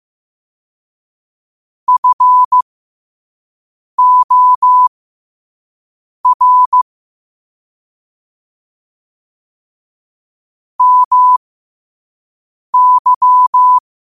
Morse code, heard as a keyed tone, about 15 wpm, spaced as for 5 wpm, 1000 Hz. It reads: FOR MY
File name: qotdfiles / 5wpm